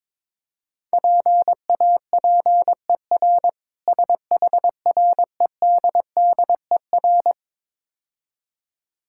Morse code, heard as a keyed tone, about 23 wpm, 700 Hz. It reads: PAPER SHREDDER
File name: Easy Words - Speed-Racing with Hints - 22wpm